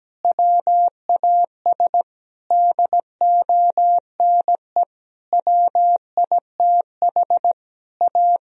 Morse code, heard as a keyed tone, about 17 wpm, 700 Hz. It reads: WAS DONE WITH A
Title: Sentences from Top 1000 Words - Review Part 4 - 17wpm